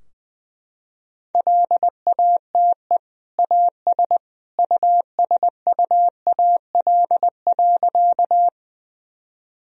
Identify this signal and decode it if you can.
Morse code, heard as a keyed tone, about 21 wpm, 700 Hz. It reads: LATE AS USUAL.
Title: Sentences from Top 1000 Words - Review Part 2 - 20wpm